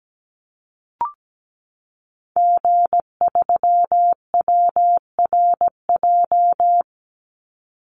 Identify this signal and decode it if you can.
Morse code, heard as a keyed tone, about 17 wpm, 700 Hz. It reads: G3WRJ
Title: Call Signs Repeated 3x - 17wpm